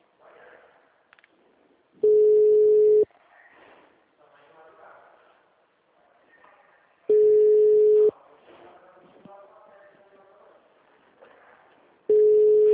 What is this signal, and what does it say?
Ringback tone (Europe, 425 Hz), with speech.
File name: out- -2022- - - .20927